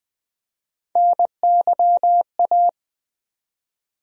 Morse code, heard as a keyed tone, about 20 wpm, 700 Hz. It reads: NYA
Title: USA Call Sign Postfixes - 3 Letters 20wpm